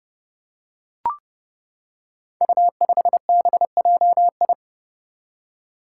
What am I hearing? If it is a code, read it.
Morse code, heard as a keyed tone, about 30 wpm, 700 Hz. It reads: U5BJI
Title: Five Challenging Letter-Number Combinations 30wpm